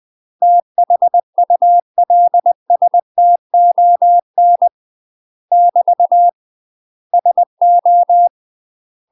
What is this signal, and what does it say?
Morse code, heard as a keyed tone, about 20 wpm, 700 Hz. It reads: THULSTON <BT> SO
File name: UK MASTER QSOs TRAINING / 20WPM / UK MASTER QSOs 0046